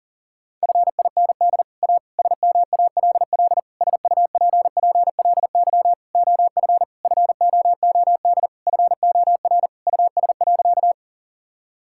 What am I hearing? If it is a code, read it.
Morse code, heard as a keyed tone, about 40 wpm, 700 Hz. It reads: FIND A SMALL SUPPLY OF FOOD FOR US.
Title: Sentences from Top 1000 Words - Review Part 5 - 40wpm